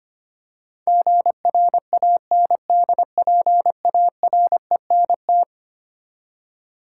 Morse code, heard as a keyed tone, about 25 wpm, 700 Hz. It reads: GRANDPARENT
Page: morse-code-ninja-practice-25wpm